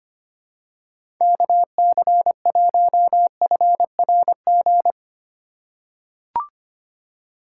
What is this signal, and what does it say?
Morse code, heard as a keyed tone, about 25 wpm, 700 Hz. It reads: KC1FRG
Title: USA Call Signs - 6 Characters - 4x Character Spacing - 25wpm